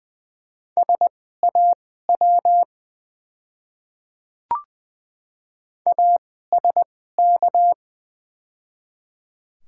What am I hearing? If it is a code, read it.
Morse code, heard as a keyed tone, about 20 wpm, spaced as for 14 wpm, 690 Hz. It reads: SAW ASK
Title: 2x Character Spacing - Top 500 Words - 20wpm